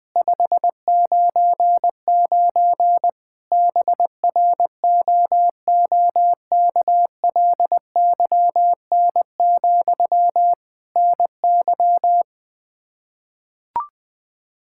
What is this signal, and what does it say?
Morse code, heard as a keyed tone, about 20 wpm, 700 Hz. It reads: 599 BROOKLYN, NY
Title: Practice QSO - Exchange 1 - Beginning - 20wpm